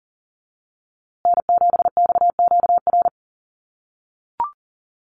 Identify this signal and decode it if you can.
Morse code, heard as a keyed tone, about 40 wpm, 700 Hz. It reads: N7XQR
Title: USA Call Signs Repeated 2x 40wpm